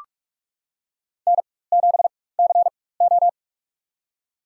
Morse code, heard as a keyed tone, about 47 wpm, spaced as for 21 wpm, 700 Hz. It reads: N7CO
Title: USA Call Signs - 4 Characters - 4x Character Spacing - 45wpm